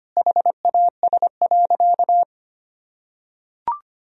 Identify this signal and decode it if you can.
Morse code, heard as a keyed tone, about 25 wpm, 700 Hz. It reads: HAS.
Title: Sentences from Top 500 Words and Abbreviations 25wpm